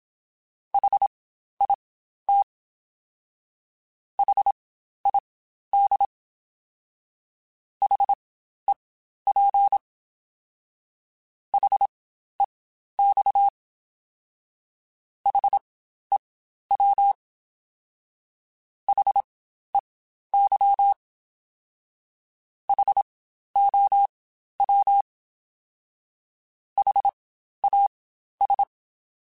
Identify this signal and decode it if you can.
Morse code, heard as a keyed tone, about 27 wpm, spaced as for 12 wpm, 800 Hz. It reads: HIT HID HEP HEX HEW HEY HOW HAS
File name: Spaced Alphas